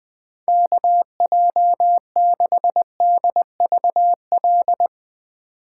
Morse code, heard as a keyed tone, about 20 wpm, 700 Hz. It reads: KJ6DVL
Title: Speed Racing - USA Call Signs 17wpm